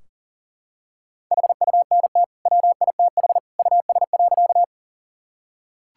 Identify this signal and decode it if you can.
Morse code, heard as a keyed tone, about 40 wpm, 700 Hz. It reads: HUNT WITH US.
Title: Sentences from Top 1600 Words 40wpm